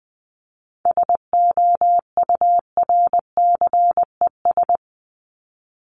Morse code, heard as a keyed tone, about 20 wpm, 700 Hz. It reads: SOURCES
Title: ES:S Suffix - Top 100 Words with Base Words 20wpm